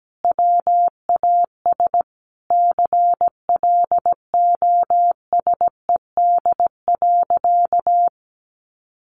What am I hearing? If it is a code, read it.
Morse code, heard as a keyed tone, about 17 wpm, 700 Hz. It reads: WAS CLOSED.
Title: Sentences from Top 1400 Words 17wpm